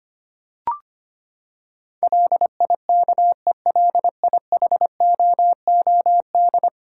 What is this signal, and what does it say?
Morse code, heard as a keyed tone, about 25 wpm, 700 Hz. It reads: LIKELIHOOD
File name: Ten Letter Words - 25wpm